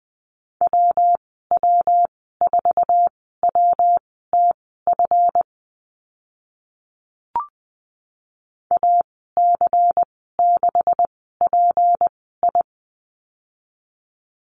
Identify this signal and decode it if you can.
Morse code, heard as a keyed tone, about 20 wpm, spaced as for 14 wpm, 700 Hz. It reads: WW4WTF AC6PI
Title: 186 Lesson 30g - Letter b - Call Signs 20wpm